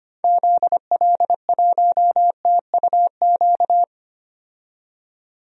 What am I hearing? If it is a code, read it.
Morse code, heard as a keyed tone, about 25 wpm, 700 Hz. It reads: ZL1TUQ
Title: DXCC-100 Call Signs - 6 Characters - 25wpm